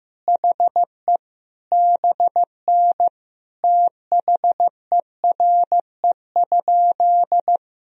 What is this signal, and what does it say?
Morse code, heard as a keyed tone, about 16 wpm, 700 Hz. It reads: HE BN THERE?
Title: Sentences from Top 100 Words and Abbreviations 15wpm